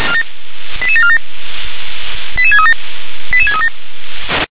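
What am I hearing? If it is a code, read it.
Five-tone selective call (ZVEI), heard as quick runs of five tones, tones of about 70 ms. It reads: 88747 88637 88637
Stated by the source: Five-tone ZVEI selective call 88786 received by SDR.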